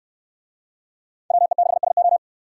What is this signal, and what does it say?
Morse code, heard as a keyed tone, about 67 wpm, 700 Hz. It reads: WE6SO